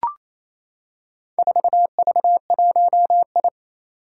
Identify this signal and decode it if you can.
Morse code, heard as a keyed tone, about 27 wpm, 700 Hz. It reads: 4V1I
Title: Four Challenging Letter-Number Combinations 28wpm